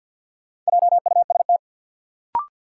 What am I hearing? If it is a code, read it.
Morse code, heard as a keyed tone, about 53 wpm, 700 Hz. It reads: JUST
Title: Sentences from Top 200 Words - New Word Drill - 50wpm